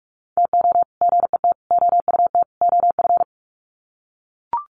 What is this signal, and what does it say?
Morse code, heard as a keyed tone, about 44 wpm, 700 Hz. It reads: TO GET OUT OF
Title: Sets of 4 Words 45wpm